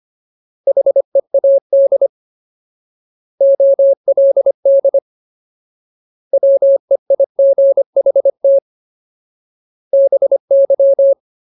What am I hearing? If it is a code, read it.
Morse code, heard as a keyed tone, about 25 wpm, 550 Hz. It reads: HEAD OLD WEIGHT BY
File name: Ultimate Word List / 25WPM Ultimate